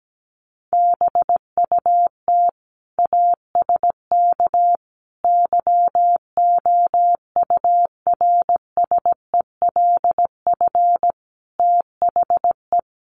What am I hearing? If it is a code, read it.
Morse code, heard as a keyed tone, about 17 wpm, 700 Hz. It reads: BUT ASK YOURSELF THE
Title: Sentences from Top 600 Words 17wpm